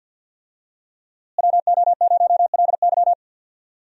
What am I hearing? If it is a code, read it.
Morse code, heard as a keyed tone, about 52 wpm, 700 Hz. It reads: WO0LY
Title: USA Call Signs - 5 Characters - Repeated 3x - 50wpm